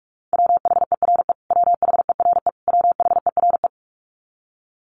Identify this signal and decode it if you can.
Morse code, heard as a keyed tone, about 44 wpm, 700 Hz. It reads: WHERE WHERE WHERE
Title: Top 300 Words Repeated 45wpm